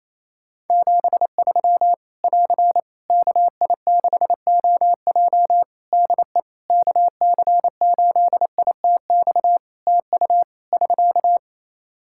Morse code, heard as a keyed tone, about 28 wpm, 700 Hz. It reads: 73 <AR> KI6OJ DE KC8ITX TU <SK>
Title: K1USN SST QSO Practice 28wpm